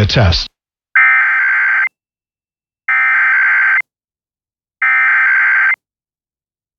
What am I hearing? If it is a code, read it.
Emergency Alert System (SAME) tones (header bursts), heard as harsh buzzing bursts, with speech. It is heard from 1.0 to 5.7 s.